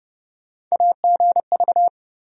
Morse code, heard as a keyed tone, about 32 wpm, 700 Hz. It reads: AGV
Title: USA Call Sign Postfixes - 3 Letters 30wpm